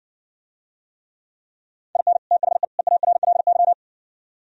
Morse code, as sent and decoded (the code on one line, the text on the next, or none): .. -. / - .... / . .- .-. .-.. -.--
IN THE EARLY